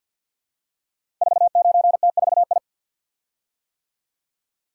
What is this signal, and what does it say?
Morse code, heard as a keyed tone, about 48 wpm, 700 Hz. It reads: 49T4I
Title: Code Groups - 5 Letter:Numbers - 50wpm